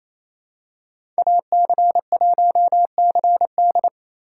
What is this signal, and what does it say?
Morse code, heard as a keyed tone, about 27 wpm, 700 Hz. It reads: AC1CD
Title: USA Call Signs - 2x Character Spacing - 28wpm